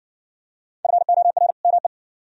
Morse code, heard as a keyed tone, about 60 wpm, 700 Hz. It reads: FOR ME